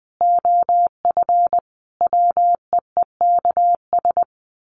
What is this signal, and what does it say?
Morse code, heard as a keyed tone, about 20 wpm, 700 Hz. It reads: OF WEEKS
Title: Sets of 5 Words Encore2 20wpm